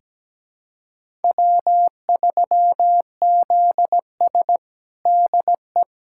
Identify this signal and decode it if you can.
Morse code, heard as a keyed tone, about 17 wpm, 700 Hz. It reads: W3ZS DE
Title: USA Callsign DE Callsign - 4 Characters - 17wpm